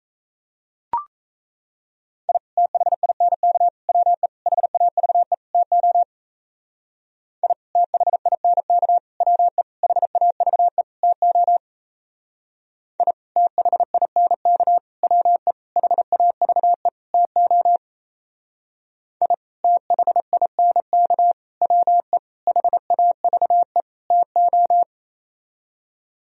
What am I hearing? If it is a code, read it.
Morse code, heard as a keyed tone, about 36 wpm, 700 Hz. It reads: I THINK WE HAVE TO I THINK WE HAVE TO I THINK WE HAVE TO I THINK WE HAVE TO